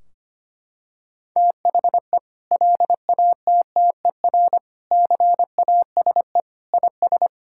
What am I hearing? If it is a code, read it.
Morse code, heard as a keyed tone, about 25 wpm, 700 Hz. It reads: THE LATTER CASE IS
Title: Sentences from Top 1600 Words 25wpm